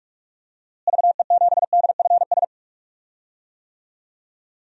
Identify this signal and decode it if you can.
Morse code, heard as a keyed tone, about 41 wpm, 700 Hz. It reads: VE7DFS
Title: DXCC-100 Call Signs - 6 Characters - Repeated 3x - 45wpm